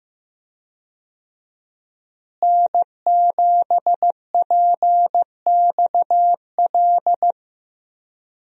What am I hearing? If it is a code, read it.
Morse code, heard as a keyed tone, about 15 wpm, 700 Hz. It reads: N7PXL
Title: USA Call Signs - 5 Characters - Repeated 3x - 15wpm